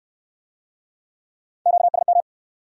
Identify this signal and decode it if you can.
Morse code, heard as a keyed tone, about 67 wpm, 700 Hz. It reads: QSM